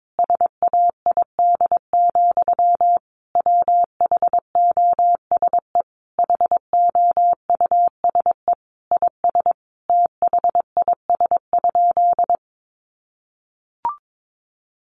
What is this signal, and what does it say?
Morse code, heard as a keyed tone, about 22 wpm, 700 Hz. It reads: SAID, WHOSE HOUSE IS THIS?
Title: Sentences from Top 1200 Words 22wpm